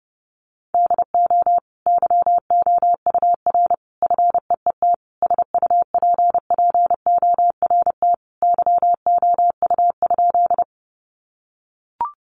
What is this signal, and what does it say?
Morse code, heard as a keyed tone, about 30 wpm, 700 Hz. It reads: DO YOUR FEET SUPPORT YOU?
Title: Sentences from Top 800 Words 30wpm